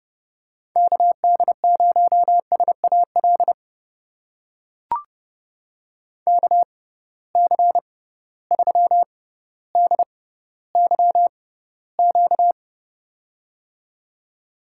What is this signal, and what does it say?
Morse code, heard as a keyed tone, about 30 wpm, spaced as for 15 wpm, 700 Hz. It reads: KD0SAL KC3DYQ
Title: USA Call Signs - 6 Characters - 6x Character Spacing - 30wpm